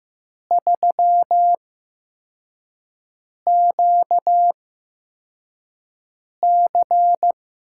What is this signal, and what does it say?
Morse code, heard as a keyed tone, about 15 wpm, spaced as for 4 wpm, 700 Hz. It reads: 3QC
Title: USA Call Signs - 4 Characters - 8x Character Spacing - 15wpm